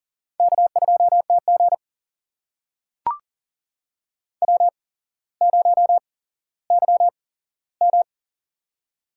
Morse code, heard as a keyed tone, about 40 wpm, spaced as for 17 wpm, 700 Hz. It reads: K2TG W0YM
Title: USA Call Signs - 4 Characters - 8x Character Spacing - 40wpm